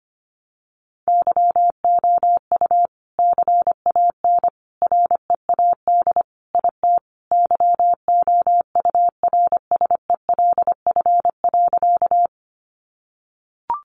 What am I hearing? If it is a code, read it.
Morse code, heard as a keyed tone, about 25 wpm, 700 Hz. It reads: YOU CAN READ IT YOURSELF.
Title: Sentences from Top 600 Words 25wpm